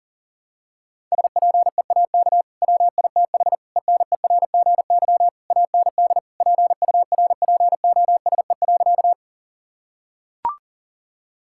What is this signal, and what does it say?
Morse code, heard as a keyed tone, about 40 wpm, 700 Hz. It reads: SPEAK WITH ENERGY AND PURPOSE.
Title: Sentences from Top 1000 Words - Review Part 6 - 40wpm